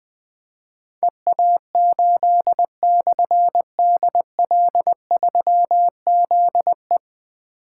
Morse code, heard as a keyed tone, about 21 wpm, 700 Hz. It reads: EA8/DL3ZE
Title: Difficult Call Signs 20wpm